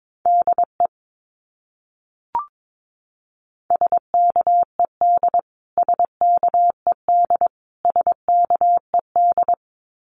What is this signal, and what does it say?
Morse code, heard as a keyed tone, about 22 wpm, 700 Hz. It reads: DE SKED SKED SKED
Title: Abbreviations Repeated 3x - 22wpm